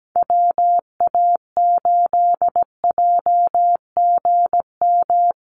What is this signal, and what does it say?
Morse code, heard as a keyed tone, about 17 wpm, 700 Hz. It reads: WA8JGM
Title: USA Call Signs Repeated 2x 17wpm